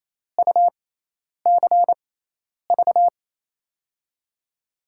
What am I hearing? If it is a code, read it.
Morse code, heard as a keyed tone, about 28 wpm, spaced as for 10 wpm, 700 Hz. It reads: UCV
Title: USA Call Signs - 6 Characters - 6x Character Spacing - 28wpm